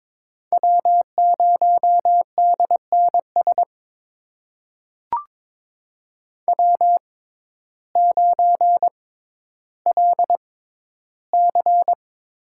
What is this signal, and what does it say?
Morse code, heard as a keyed tone, about 22 wpm, spaced as for 12 wpm, 700 Hz. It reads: W0DNS W9LC
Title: USA Call Signs - 5 Characters - 6x Character Spacing - 22wpm